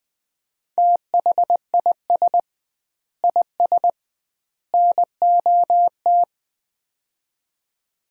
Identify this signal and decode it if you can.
Morse code, heard as a keyed tone, about 20 wpm, 700 Hz. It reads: THIS IS NOT
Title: 2x Word-Spacing - Sets of 3 Words - 20wpm